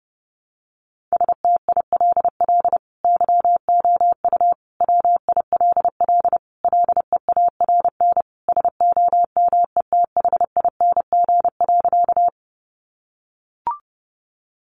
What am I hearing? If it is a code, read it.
Morse code, heard as a keyed tone, about 30 wpm, 700 Hz. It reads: STILL YOU WILL LEARN SOMETHING.
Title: Sentences from Top 600 Words 30wpm